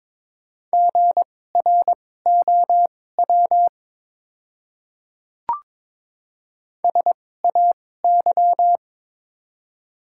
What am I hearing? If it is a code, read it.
Morse code, heard as a keyed tone, about 22 wpm, spaced as for 16 wpm, 700 Hz. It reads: GROW SAY
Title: 2x Character Spacing - Top 300 Words - 22wpm